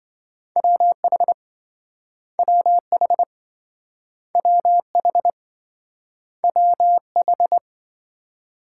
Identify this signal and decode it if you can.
Morse code, heard as a keyed tone, about 28 wpm, 700 Hz. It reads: WH WH WH WH